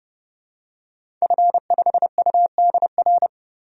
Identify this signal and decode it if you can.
Morse code, heard as a keyed tone, about 30 wpm, 700 Hz. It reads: F5UDR
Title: DXCC-100 Call Signs - 5 Characters - 30wpm